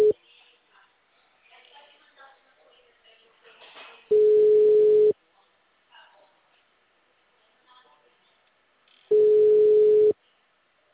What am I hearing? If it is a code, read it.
Ringback tone (Europe, 425 Hz), with speech.